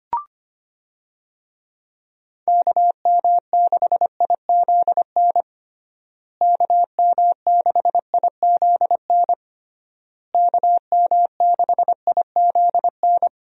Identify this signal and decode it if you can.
Morse code, heard as a keyed tone, about 25 wpm, 700 Hz. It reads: KM6IZN KM6IZN KM6IZN
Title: USA Call Signs - 6 Characters - Repeated 3x - 25wpm